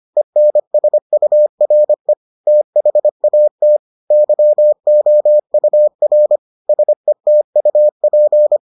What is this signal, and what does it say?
Morse code, heard as a keyed tone, about 25 wpm, 600 Hz. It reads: ENSURE THAT YOUR SETUP